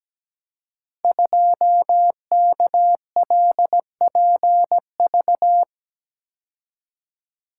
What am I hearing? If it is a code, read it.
Morse code, heard as a keyed tone, about 17 wpm, 700 Hz. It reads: 2KLPV
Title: Code Groups - 5 Letter:Numbers - 17wpm